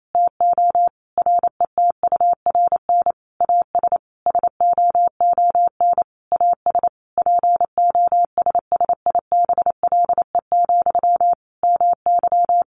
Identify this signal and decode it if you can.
Morse code, heard as a keyed tone, about 28 wpm, 700 Hz. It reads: TO RETURN AS SOON AS POSSIBLE, MY
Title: Sentences from Top 800 Words 28wpm